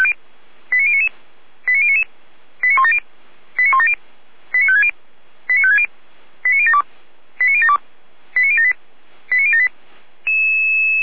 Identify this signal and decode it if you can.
Five-tone selective call (ZVEI), heard as quick runs of five tones, tones of about 70 ms. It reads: 89099 89099 89170 89170 89560 89560 89072 89072 89078 89078